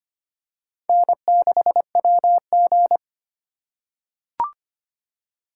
Morse code, heard as a keyed tone, about 25 wpm, 700 Hz. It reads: N6WG
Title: K1USN SST QSO Practice - 25wpm